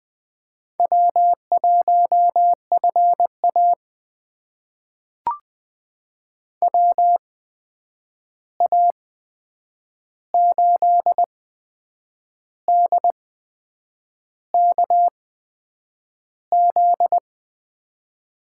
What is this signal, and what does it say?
Morse code, heard as a keyed tone, about 20 wpm, spaced as for 8 wpm, 700 Hz. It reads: W1FA WA8DKZ
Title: USA Call Signs - 8x Character Spacing - 20wpm